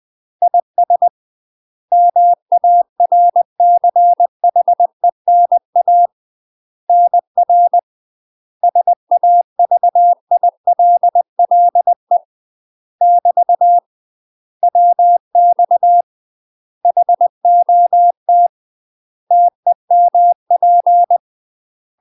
Morse code, heard as a keyed tone, about 20 wpm, 700 Hz. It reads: IS MARCHENA NR SAVILLE <BT> WX HOT TEMP
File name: UK MASTER QSOs TRAINING / 20WPM / UK MASTER QSOs 0060